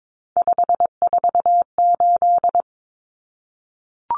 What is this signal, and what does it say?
Morse code, heard as a keyed tone, about 22 wpm, 700 Hz. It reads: 548
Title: Numbers - 3-Digit 22wpm